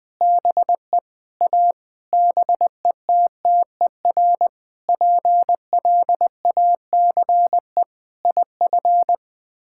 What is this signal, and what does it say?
Morse code, heard as a keyed tone, about 19 wpm, 700 Hz. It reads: BE A BETTER PLACE IF
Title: Sets of 8 Words - 20wpm